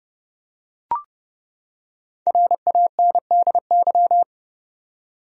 Morse code, heard as a keyed tone, about 30 wpm, 700 Hz. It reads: RANDY
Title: USA Common Names - Max 5 Letters - 30wpm